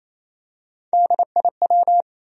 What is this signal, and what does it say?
Morse code, heard as a keyed tone, about 28 wpm, 700 Hz. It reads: DIW
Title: USA Call Sign Postfixes - 3 Letters 28wpm